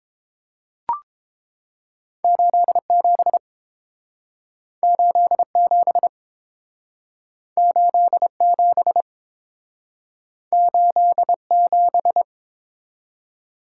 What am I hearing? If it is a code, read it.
Morse code, heard as a keyed tone, about 30 wpm, 700 Hz. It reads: 87 87 87 87